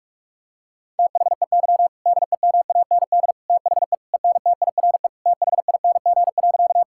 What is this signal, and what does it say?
Morse code, heard as a keyed tone, about 46 wpm, 700 Hz. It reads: THEY DEMAND THE ENTIRE THING.
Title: Sentences from Top 1000 Words - Review Part 5 - 45wpm